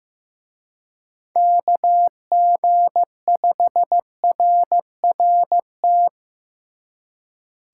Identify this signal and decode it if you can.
Morse code, heard as a keyed tone, about 18 wpm, 700 Hz. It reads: KG5RRT
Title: Speed Racing - USA Call Signs 15wpm